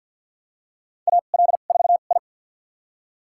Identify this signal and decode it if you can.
Morse code, heard as a keyed tone, about 52 wpm, spaced as for 37 wpm, 700 Hz. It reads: AL4I
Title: USA Call Signs - 4 Characters - 2x Character Spacing - 50wpm